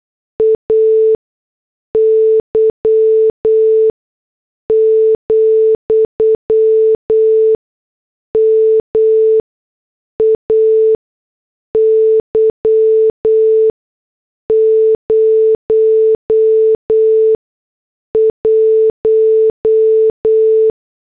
Morse code, heard as a keyed tone, about 8 wpm, spaced as for 6 wpm, 440 Hz. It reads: AY, MAY 01